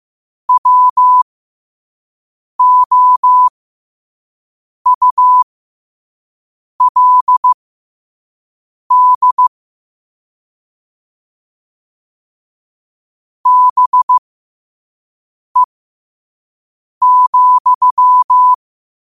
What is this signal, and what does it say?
Morse code, heard as a keyed tone, about 15 wpm, spaced as for 5 wpm, 1000 Hz. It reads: WOULD BE,
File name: qotdfiles / 5wpm